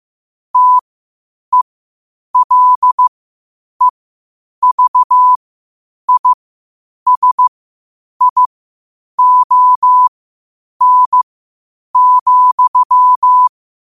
Morse code, heard as a keyed tone, about 15 wpm, spaced as for 8 wpm, 1000 Hz. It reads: TELEVISION,
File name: qotdfiles / 8wpm